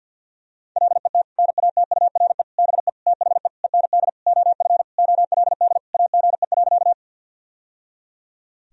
Morse code, heard as a keyed tone, about 52 wpm, 700 Hz. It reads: LET NATURE BE THE END OF OLD AGE.
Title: Sentences from Top 1000 Words - Review Part 3 - 50wpm